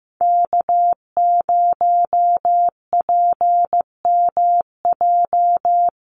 Morse code, heard as a keyed tone, about 15 wpm, 700 Hz. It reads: K0PMJ
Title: USA Call Signs Repeated 3x 15wpm